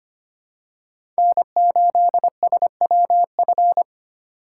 Morse code, heard as a keyed tone, about 25 wpm, 700 Hz. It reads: N8SWF
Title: USA Call Signs - 5 Characters - 4x Character Spacing - 25wpm